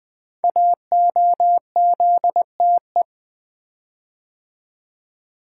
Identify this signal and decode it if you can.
Morse code, heard as a keyed tone, about 20 wpm, 700 Hz. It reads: AOZTE
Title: Code Groups - 5 Letters - 20wpm